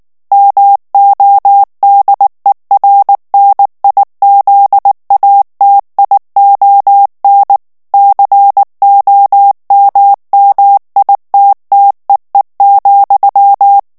Morse code, heard as a keyed tone, about 19 wpm, 800 Hz. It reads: MODERNIZATION COMMITTEE,
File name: ANGLAIS texte clair 3 vitesse 18 mots minute